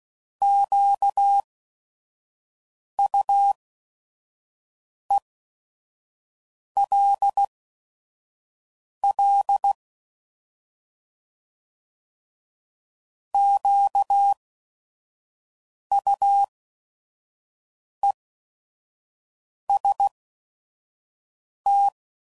Morse code, heard as a keyed tone, about 16 wpm, spaced as for 5 wpm, 800 Hz. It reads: QUELL QUEST
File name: Prefixes / By First Letter / Qs